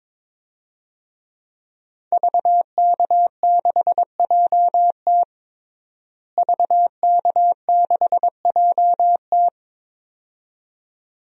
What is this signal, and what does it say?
Morse code, heard as a keyed tone, about 22 wpm, 700 Hz. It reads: VK6JT VK6JT
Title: DXCC-100 Call Signs - 5 Characters - Repeated 2x - 22wpm